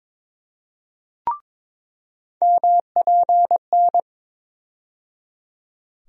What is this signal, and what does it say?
Morse code, heard as a keyed tone, about 22 wpm, 700 Hz. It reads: MPN
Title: USA Call Sign Postfixes - 3 Letters 22wpm